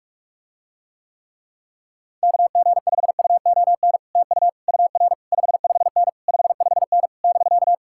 Morse code, heard as a keyed tone, about 49 wpm, 700 Hz. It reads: KG5UON TU UR 55N 55N <BK>